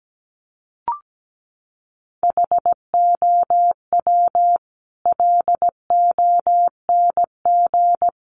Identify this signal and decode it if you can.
Morse code, heard as a keyed tone, about 17 wpm, 700 Hz. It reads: HOW LONG
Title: Sets of 5 Words Encore 17wpm